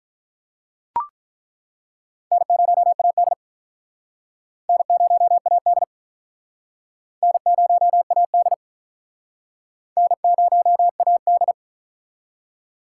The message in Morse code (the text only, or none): N0AD N0AD N0AD N0AD